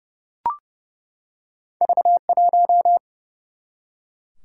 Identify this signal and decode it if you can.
Morse code, heard as a keyed tone, about 32 wpm, 700 Hz. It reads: V1
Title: Single-Letter Single-Number 30wpm